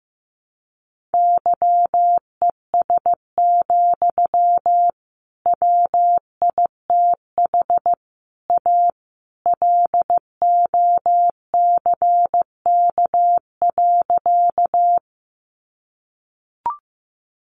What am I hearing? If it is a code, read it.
Morse code, heard as a keyed tone, about 15 wpm, 700 Hz. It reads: YES, WITH A LOCK.
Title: Sentences from Top 1300 Words 15wpm